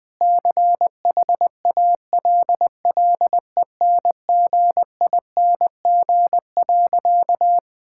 Morse code, heard as a keyed tone, about 20 wpm, 700 Hz. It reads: CHALLENGING.
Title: Sentences with 5-Word Phrases - 20wpm